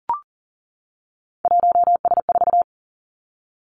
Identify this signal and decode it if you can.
Morse code, heard as a keyed tone, about 41 wpm, 700 Hz. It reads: 1S4
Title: Three Challenging Letter-Number Combinations 40wpm